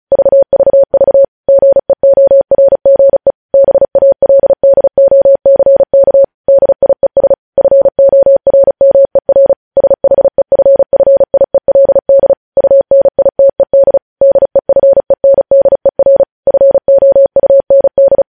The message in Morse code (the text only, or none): VVV GEORGE BALDOCK DIES FORMER SHEFFIELD UNITED DEFENDER FOUND